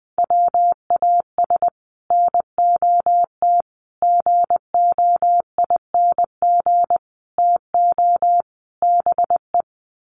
Morse code, heard as a keyed tone, about 20 wpm, 700 Hz. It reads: WAS NOT GOING TO BE
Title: Sets of 5 Words Encore 20wpm